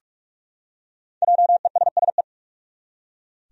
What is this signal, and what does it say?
Morse code, heard as a keyed tone, about 44 wpm, 700 Hz. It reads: JESSE
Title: USA Common Names - Max 5 Letters - 45wpm